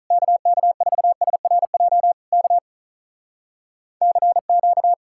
Morse code, heard as a keyed tone, about 46 wpm, 700 Hz. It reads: KK4SRJ K CQ